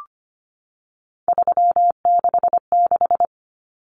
Morse code, heard as a keyed tone, about 25 wpm, 700 Hz. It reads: 366
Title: Common Cut Numbers - 3-Digit 25wpm